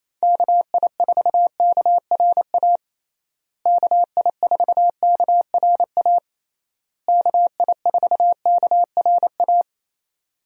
Morse code, heard as a keyed tone, about 27 wpm, 700 Hz. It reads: KI4KRA KI4KRA KI4KRA
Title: USA Call Signs - 6 Characters - Repeated 3x - 28wpm